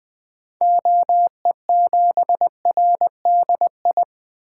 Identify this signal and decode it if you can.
Morse code, heard as a keyed tone, about 20 wpm, 700 Hz. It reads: OE7RDI
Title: 2023 SOTA Activator Call Signs - 20wpm